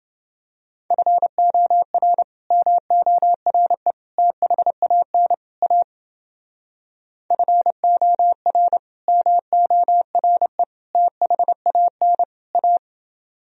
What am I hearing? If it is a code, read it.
Morse code, heard as a keyed tone, about 29 wpm, 700 Hz. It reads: FOR MORE THAN A FOR MORE THAN A